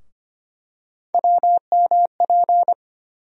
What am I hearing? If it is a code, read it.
Morse code, heard as a keyed tone, about 25 wpm, 700 Hz. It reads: WMP
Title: USA Call Sign Postfixes - 3 Letters 25wpm